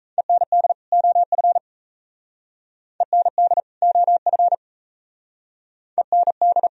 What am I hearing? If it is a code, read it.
Morse code, heard as a keyed tone, about 35 wpm, 700 Hz. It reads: END OF END OF END